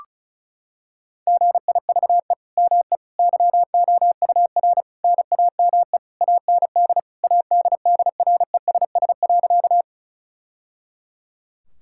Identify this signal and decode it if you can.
Morse code, heard as a keyed tone, about 36 wpm, 700 Hz. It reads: GIVE ME YOUR NAME AND ADDRESS.
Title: Sentences from Top 1000 Words - Review Part 6 - 35wpm